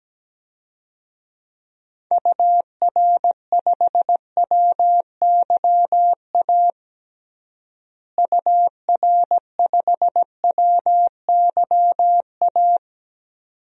Morse code, heard as a keyed tone, about 17 wpm, 700 Hz. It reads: UR5WYA UR5WYA
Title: DXCC-100 Call Signs - 6 Characters - Repeated 2x - 17wpm